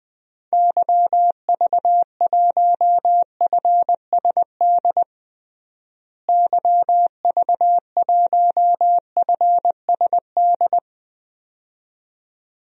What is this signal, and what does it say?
Morse code, heard as a keyed tone, about 20 wpm, 700 Hz. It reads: YV1FSD YV1FSD
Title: DXCC-100 Call Signs - 6 Characters - Repeated 2x - 20wpm